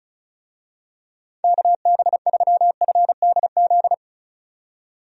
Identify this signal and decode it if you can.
Morse code, heard as a keyed tone, about 36 wpm, 700 Hz. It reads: KB3FDZ
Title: USA Call Signs - 6 Characters - 35wpm